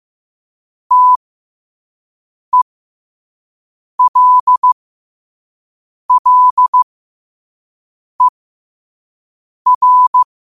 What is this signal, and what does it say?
Morse code, heard as a keyed tone, about 15 wpm, spaced as for 5 wpm, 1000 Hz. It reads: TELLER
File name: qotdfiles / 5wpm